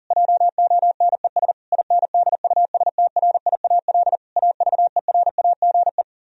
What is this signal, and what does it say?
Morse code, heard as a keyed tone, about 40 wpm, 700 Hz. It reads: JONES INDUSTRIAL AVERAGE